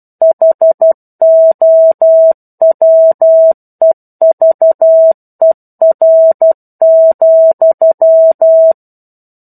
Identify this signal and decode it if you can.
Morse code, heard as a keyed tone, about 12 wpm, 650 Hz. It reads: HOWEVER,
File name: Chapter-01 12 WPM